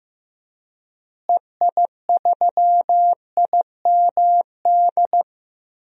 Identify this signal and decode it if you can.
Morse code, heard as a keyed tone, about 16 wpm, 700 Hz. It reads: EI3IMD
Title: DXCC-100 Call Signs - 6 Characters - Repeated 2x - 15wpm